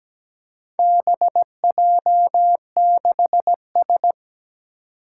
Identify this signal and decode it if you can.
Morse code, heard as a keyed tone, about 17 wpm, 700 Hz. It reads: BJ6S
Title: Four Challenging Letter-Number Combinations 17wpm